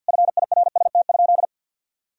Morse code, heard as a keyed tone, about 50 wpm, 700 Hz. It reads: FIRST?
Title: Sentences from Top 1000 Words - Review Part 3 - 50wpm